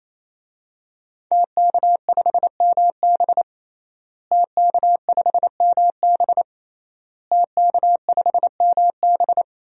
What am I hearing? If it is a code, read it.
Morse code, heard as a keyed tone, about 28 wpm, 700 Hz. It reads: TK5MB TK5MB TK5MB
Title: DXCC-100 Call Signs - 5 Characters - Repeated 3x - 28wpm